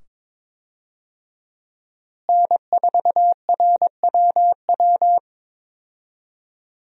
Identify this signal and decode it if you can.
Morse code, heard as a keyed tone, about 22 wpm, 700 Hz. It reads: N4RWW
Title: USA Call Signs - 5 Characters - Repeated 2x - 22wpm